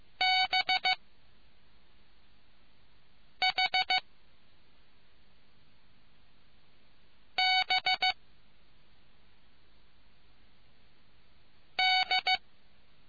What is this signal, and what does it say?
Morse code, heard as a keyed tone, about 15 wpm, 750 Hz. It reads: B H B D